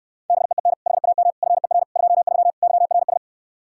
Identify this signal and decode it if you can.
Morse code, heard as a keyed tone, about 68 wpm, 700 Hz. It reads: BEEN HAM FER 23 YRS